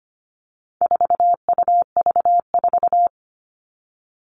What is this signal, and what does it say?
Morse code, heard as a keyed tone, about 25 wpm, 700 Hz. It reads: 4UV4
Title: All Cut Numbers - 4-Digit 25wpm